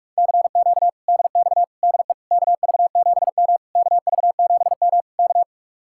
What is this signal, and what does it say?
Morse code, heard as a keyed tone, about 46 wpm, 700 Hz. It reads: CQ DX DE KV7M KV7M K